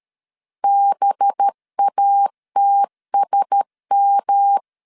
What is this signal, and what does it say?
Morse code, heard as a keyed tone, about 13 wpm, 800 Hz. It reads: BATSM
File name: LEZIONI / 8 YJ45 / 8 Y 12wpm 2228